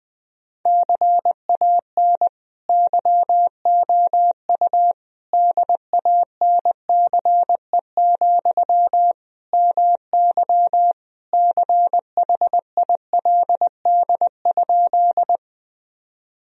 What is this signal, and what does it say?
Morse code, heard as a keyed tone, about 21 wpm, 700 Hz. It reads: CAN YOU DANCE, MY CHILD?
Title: Sentences from Top 1000 Words - Review Part 4 - 20wpm